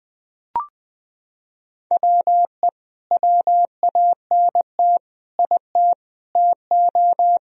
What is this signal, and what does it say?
Morse code, heard as a keyed tone, about 20 wpm, 700 Hz. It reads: WE WANT IT TO
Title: Sentences from Top 1000 Words - Review Part 1 - 20wpm